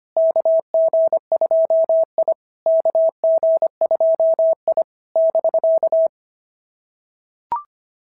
Morse code, heard as a keyed tone, about 25 wpm, 690 Hz. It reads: KG2I KG2I <BK>
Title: SOTA QSO Practice 25wpm